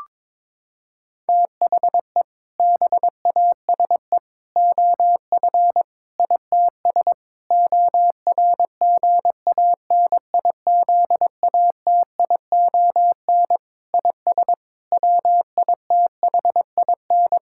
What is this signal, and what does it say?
Morse code, heard as a keyed tone, about 22 wpm, 700 Hz. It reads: THE BASE OF ITS ORGANIZATION IS WITHIN
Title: Sentences from Top 1000 Words - Review Part 5 - 22wpm